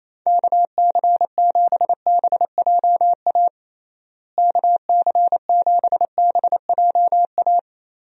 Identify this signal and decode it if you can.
Morse code, heard as a keyed tone, about 28 wpm, 700 Hz. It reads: KC7BJA KC7BJA
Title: USA Call Signs - 6 Characters - Repeated 3x - 28wpm